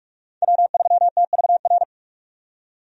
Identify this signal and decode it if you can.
Morse code, heard as a keyed tone, about 42 wpm, 700 Hz. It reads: W3TVR
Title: USA Call Signs - 5 Characters - 45wpm